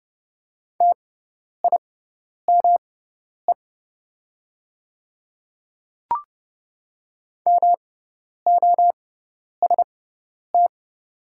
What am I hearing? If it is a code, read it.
Morse code, heard as a keyed tone, about 30 wpm, spaced as for 10 wpm, 700 Hz. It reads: TIME MOST
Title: 6x Character Spacing - Top 100 Words - 30wpm